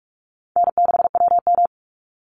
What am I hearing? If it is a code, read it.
Morse code, heard as a keyed tone, about 44 wpm, 700 Hz. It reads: N6WM
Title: Easy Call Signs Repeated 3x - 45wpm